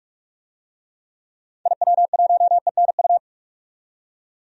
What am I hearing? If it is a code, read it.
Morse code, heard as a keyed tone, about 46 wpm, 700 Hz. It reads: IW1ENU
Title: DXCC-100 Call Signs - 6 Characters - 45wpm